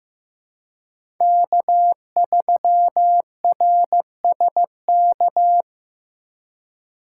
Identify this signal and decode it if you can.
Morse code, heard as a keyed tone, about 15 wpm, 700 Hz. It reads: K3RSK
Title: USA Call Signs - 5 Characters - 4x Character Spacing - 15wpm